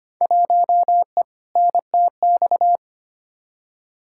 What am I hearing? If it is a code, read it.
Morse code, heard as a keyed tone, about 25 wpm, 700 Hz. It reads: 1E NTX
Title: Field Day Exchanges - 25wpm